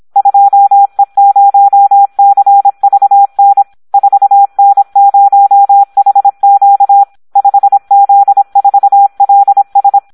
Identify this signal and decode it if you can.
Morse code, heard as a keyed tone, about 26 wpm, 800 Hz. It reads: 2E0CVN 4N0HQ 5Z4LS
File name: Audio practica CW / SD 1 26